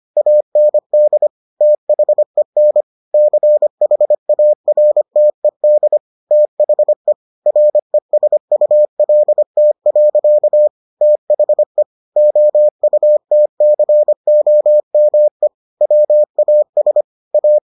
Morse code, heard as a keyed tone, about 25 wpm, 600 Hz. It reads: AND THEN CHARTED THE RESULT. THE OUTCOME WAS A